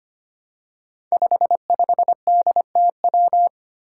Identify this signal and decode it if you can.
Morse code, heard as a keyed tone, about 25 wpm, 700 Hz. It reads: 55DTW
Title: Five Challenging Letter-Number Combinations 25wpm